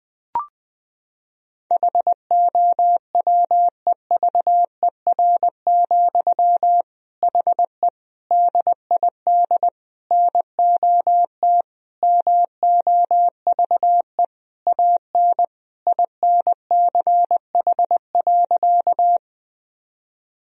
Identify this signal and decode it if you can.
Morse code, heard as a keyed tone, about 20 wpm, 700 Hz. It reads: HOWEVER, HE DID NOT MOVE AN INCH.
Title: Sentences from Top 1000 Words - Review Part 3 - 20wpm